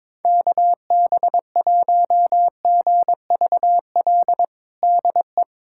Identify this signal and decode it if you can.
Morse code, heard as a keyed tone, about 22 wpm, 700 Hz. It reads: KB1GVL DE
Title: USA Callsign DE Callsign - 6 Characters - 22wpm